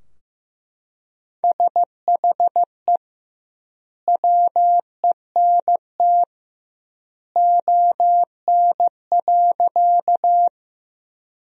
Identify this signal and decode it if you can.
Morse code, heard as a keyed tone, about 15 wpm, 700 Hz. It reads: SHE WENT ON.
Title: 2x Word-Spacing - Sentences from Top 200 Words - 15wpm